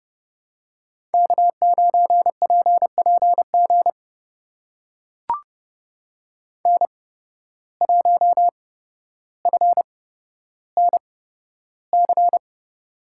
Morse code, heard as a keyed tone, about 30 wpm, spaced as for 13 wpm, 700 Hz. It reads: K9PPG N1FNC
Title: USA Call Signs - 5 Characters - 8x Character Spacing - 30wpm